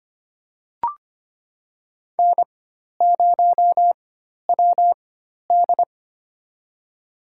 Morse code, heard as a keyed tone, about 25 wpm, spaced as for 12 wpm, 700 Hz. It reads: N0WD
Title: USA Call Signs - 4 Characters - 4x Character Spacing - 25wpm